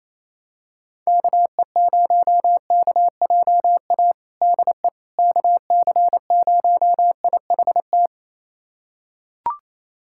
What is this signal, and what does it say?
Morse code, heard as a keyed tone, about 28 wpm, 700 Hz. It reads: KE0KJA DE KC0IHT
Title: USA Callsign DE Callsign - 6 Characters - 28wpm